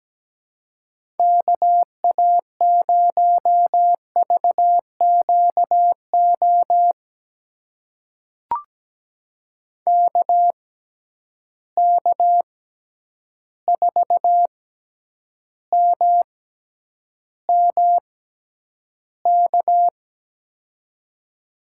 Morse code, heard as a keyed tone, about 17 wpm, spaced as for 9 wpm, 700 Hz. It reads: KA0VQO KK4MMK
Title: USA Call Signs - 6x Character Spacing - 17wpm